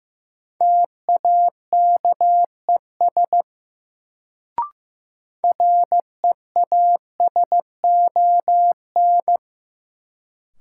Morse code, heard as a keyed tone, about 15 wpm, 700 Hz. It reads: TAKES REASON
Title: Sets of 5 Words Encore - New Word Drill - 15wpm